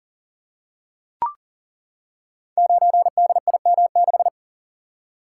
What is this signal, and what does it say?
Morse code, heard as a keyed tone, about 42 wpm, 700 Hz. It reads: 9DIM6
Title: Code Groups - 5 Letter:Numbers - 40wpm